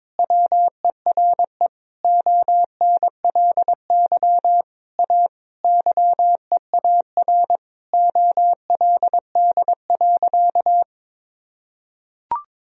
Morse code, heard as a keyed tone, about 22 wpm, 700 Hz. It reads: WERE ONLY A YEAR OLD.
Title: Sentences from Top 1000 Words - Review Part 2 - 22wpm